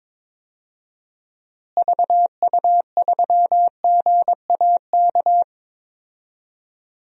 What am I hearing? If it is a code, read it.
Morse code, heard as a keyed tone, about 21 wpm, 700 Hz. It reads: VU3GAK
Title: DXCC-100 Call Signs - 6 Characters - Repeated 2x - 22wpm